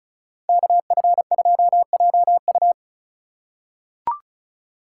Morse code, heard as a keyed tone, about 34 wpm, 700 Hz. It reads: KF2JU
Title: USA Call Signs - 5 Characters - 2x Character Spacing - 35wpm